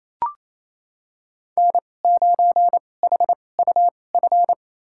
Morse code, heard as a keyed tone, about 28 wpm, spaced as for 20 wpm, 700 Hz. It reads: N9HUF
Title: USA Call Signs - 2x Character Spacing - 28wpm